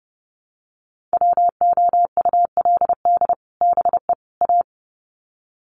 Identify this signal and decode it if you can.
Morse code, heard as a keyed tone, about 30 wpm, 700 Hz. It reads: WOULD BE A
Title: Sets of 3 Words 30wpm